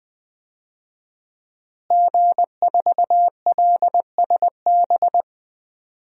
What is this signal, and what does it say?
Morse code, heard as a keyed tone, about 20 wpm, 700 Hz. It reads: G4LSB
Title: DXCC-100 Call Signs - 5 Characters - Repeated 3x - 20wpm